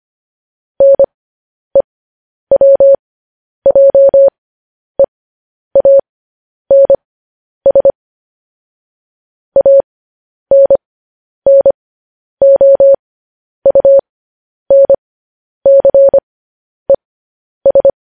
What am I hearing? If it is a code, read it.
Morse code, heard as a keyed tone, about 25 wpm, spaced as for 10 wpm, 550 Hz. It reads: NEWJEANS ANNOUNCES
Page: news-headlines-in-morse-code-at-10-wpm